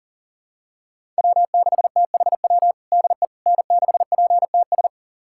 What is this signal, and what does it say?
Morse code, heard as a keyed tone, about 40 wpm, 700 Hz. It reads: W6THW DE N6PTS
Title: USA Callsign DE Callsign - 5 Characters - 40wpm